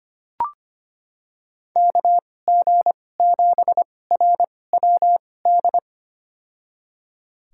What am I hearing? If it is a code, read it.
Morse code, heard as a keyed tone, about 25 wpm, spaced as for 18 wpm, 700 Hz. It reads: KG7RWD
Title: USA Call Signs - 2x Character Spacing - 25wpm